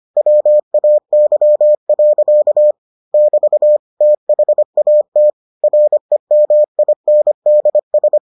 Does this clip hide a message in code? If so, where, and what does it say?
Yes: Morse code from 0.2 to 8.2 s, heard as a keyed tone, about 25 wpm, 600 Hz. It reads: WAY. <BT> THAT REMINDS